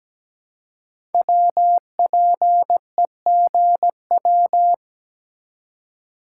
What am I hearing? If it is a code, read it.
Morse code, heard as a keyed tone, about 17 wpm, 700 Hz. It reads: WPEGW
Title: Code Groups - 5 Letter:Numbers - 17wpm